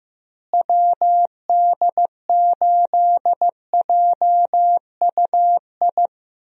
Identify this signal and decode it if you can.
Morse code, heard as a keyed tone, about 15 wpm, 700 Hz. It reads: WD8JUI
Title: USA Call Signs - 6 Characters - Repeated 3x - 15wpm